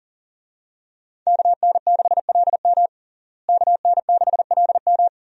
Morse code, heard as a keyed tone, about 37 wpm, 700 Hz. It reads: KN6LM KN6LM
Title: USA Call Signs - 5 Characters - Repeated 3x - 40wpm